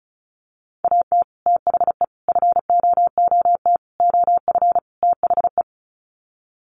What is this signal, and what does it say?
Morse code, heard as a keyed tone, about 34 wpm, 700 Hz. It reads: AT THE FOOT OF THE
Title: Sets of 5 Words 35wpm